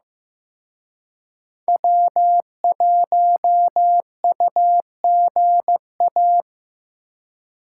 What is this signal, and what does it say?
Morse code, heard as a keyed tone, about 18 wpm, 700 Hz. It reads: W1UGA
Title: Speed Racing - USA Call Signs 15wpm